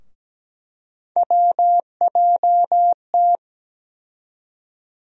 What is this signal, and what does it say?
Morse code, heard as a keyed tone, about 17 wpm, 700 Hz. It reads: WJT
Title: USA Call Sign Postfixes - 3 Letters 17wpm